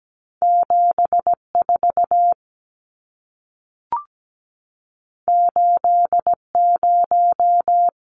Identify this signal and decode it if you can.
Morse code, heard as a keyed tone, about 17 wpm, 700 Hz. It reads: 74 80
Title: Numbers - 2-Digit 17wpm